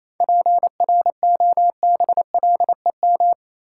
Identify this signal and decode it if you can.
Morse code, heard as a keyed tone, about 28 wpm, 700 Hz. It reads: PROBLEM
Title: Sentences from Top 1000 Words - New Word Drill - 28wpm